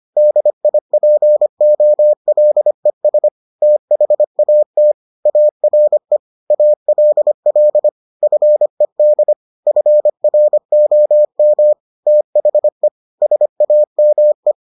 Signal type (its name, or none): Morse code